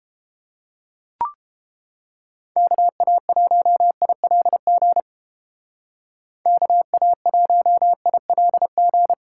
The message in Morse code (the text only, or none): KA1ILG KA1ILG